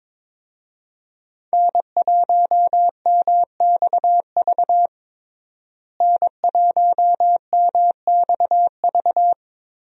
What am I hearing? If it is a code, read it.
Morse code, heard as a keyed tone, about 22 wpm, 700 Hz. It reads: N1MXV N1MXV
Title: USA Call Signs - 5 Characters - Repeated 3x - 22wpm